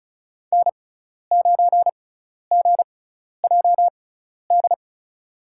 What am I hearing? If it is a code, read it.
Morse code, heard as a keyed tone, about 36 wpm, spaced as for 12 wpm, 700 Hz. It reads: N9GJD
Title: USA Call Signs - 5 Characters - 6x Character Spacing - 35wpm